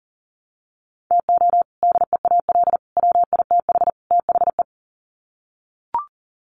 Morse code, heard as a keyed tone, about 40 wpm, 700 Hz. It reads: TO DEAL WITH THE
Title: Sets of 4 Words 40wpm